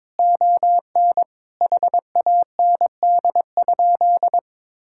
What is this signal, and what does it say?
Morse code, heard as a keyed tone, about 22 wpm, 700 Hz. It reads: ON HAND?
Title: Sentences from Top 1000 Words - Review Part 5 - 22wpm